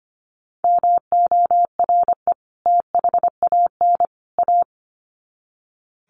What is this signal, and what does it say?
Morse code, heard as a keyed tone, about 25 wpm, 700 Hz. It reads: MORE THAN A
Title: Sets of 3 Words 25wpm